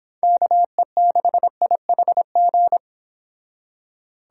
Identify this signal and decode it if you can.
Morse code, heard as a keyed tone, about 25 wpm, 700 Hz. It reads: KE6IHG